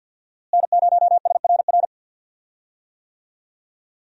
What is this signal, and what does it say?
Morse code, heard as a keyed tone, about 52 wpm, 700 Hz. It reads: N0SRR
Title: USA Call Signs - 5 Characters - Repeated 2x - 50wpm